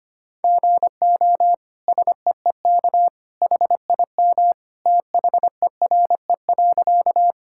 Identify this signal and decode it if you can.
Morse code, heard as a keyed tone, about 25 wpm, 700 Hz. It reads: GO SEEK HIM THERE.
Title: Sentences from Top 1600 Words 25wpm